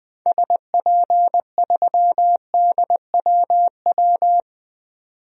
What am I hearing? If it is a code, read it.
Morse code, heard as a keyed tone, about 20 wpm, 700 Hz. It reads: SP3DWW
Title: DXCC-100 Call Signs - 6 Characters - 20wpm